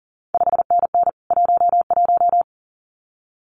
Morse code, heard as a keyed tone, about 40 wpm, 700 Hz. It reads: 5NN 11
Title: CQ WW DX Contest Exchange Encore - 40wpm